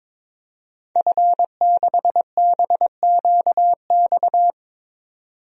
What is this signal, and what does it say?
Morse code, heard as a keyed tone, about 22 wpm, 700 Hz. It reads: F6BQX
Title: DXCC-100 Call Signs - 5 Characters - 22wpm